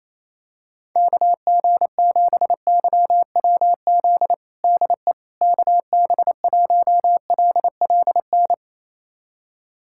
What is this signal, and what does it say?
Morse code, heard as a keyed tone, about 28 wpm, 700 Hz. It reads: KG7YWZ DE KB1LLN
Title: USA Callsign DE Callsign - 6 Characters - 28wpm